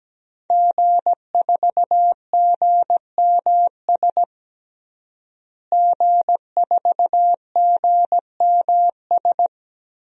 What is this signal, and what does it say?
Morse code, heard as a keyed tone, about 17 wpm, 700 Hz. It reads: G4GMS G4GMS
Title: DXCC-100 Call Signs - 5 Characters - Repeated 3x - 17wpm